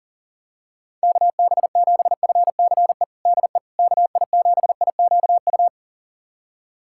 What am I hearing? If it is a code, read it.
Morse code, heard as a keyed tone, about 40 wpm, 700 Hz. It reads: KB7FCE DE KI7IQU
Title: USA Callsign DE Callsign - 6 Characters - 40wpm